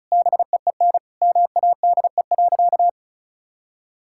Morse code, heard as a keyed tone, about 36 wpm, 690 Hz. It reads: BEEN MADE.
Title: Sentences from Top 1000 Words - Review Part 5 - 35wpm